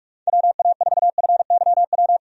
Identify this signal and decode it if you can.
Morse code, heard as a keyed tone, about 47 wpm, 700 Hz. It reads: WA4FYW
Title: USA Call Signs - 6 Characters - 45wpm